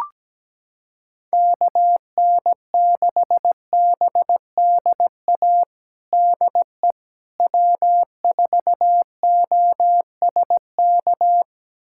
Morse code, heard as a keyed tone, about 17 wpm, 700 Hz. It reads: KN6BDA DE W4OSK
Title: USA Callsign DE Callsign 17wpm